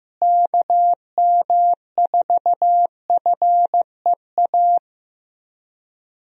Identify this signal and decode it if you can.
Morse code, heard as a keyed tone, about 15 wpm, 700 Hz. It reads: KM4FEA
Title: USA Call Signs - 6 Characters - Repeated 3x - 15wpm